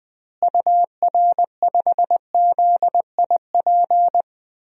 Morse code, heard as a keyed tone, about 21 wpm, 700 Hz. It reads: UR5ZIP
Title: DXCC-100 Call Signs - 6 Characters - Repeated 3x - 20wpm